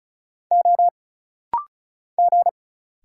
Morse code, heard as a keyed tone, about 36 wpm, 700 Hz. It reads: O G
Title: Single Letters - Rapid-Fire 35wpm